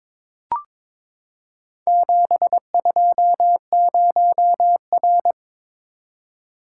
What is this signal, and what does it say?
Morse code, heard as a keyed tone, about 23 wpm, 700 Hz. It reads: 720R
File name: practice / 22wpm / Four Letter-Number Combinations - 22wpm